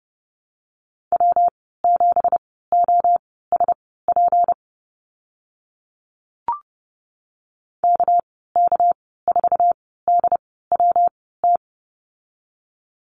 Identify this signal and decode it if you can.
Morse code, heard as a keyed tone, about 30 wpm, spaced as for 17 wpm, 700 Hz. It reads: W7OSP KK4DWT
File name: Lesson 37g - Letter x - Call Signs 30wpm